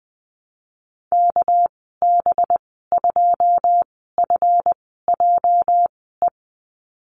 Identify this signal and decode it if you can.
Morse code, heard as a keyed tone, about 20 wpm, spaced as for 15 wpm, 700 Hz. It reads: KB2FJE
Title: 219 Lesson 35g - Number 0 - Call Signs 20wpm